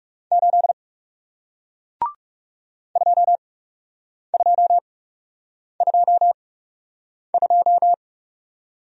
Morse code, heard as a keyed tone, about 41 wpm, spaced as for 8 wpm, 700 Hz. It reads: 8 2222